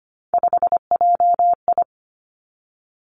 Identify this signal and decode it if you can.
Morse code, heard as a keyed tone, about 25 wpm, 700 Hz. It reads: 5JI
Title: Three Challenging Letter-Number Combinations 25wpm